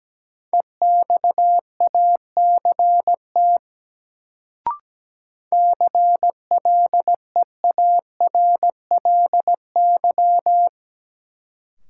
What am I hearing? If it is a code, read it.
Morse code, heard as a keyed tone, about 17 wpm, 700 Hz. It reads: EXACT CLEARLY
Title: Sentences from Top 1300 Words - New Word Drill - 17wpm